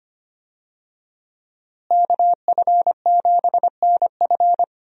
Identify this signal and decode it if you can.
Morse code, heard as a keyed tone, about 25 wpm, 700 Hz. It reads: KF7NF
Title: USA Call Signs - 5 Characters - Repeated 3x - 25wpm